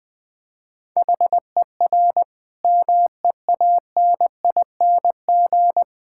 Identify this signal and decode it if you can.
Morse code, heard as a keyed tone, about 20 wpm, 700 Hz. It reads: HER MEANING
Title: Sentences from Top 1000 Words - Review Part 6 - 20wpm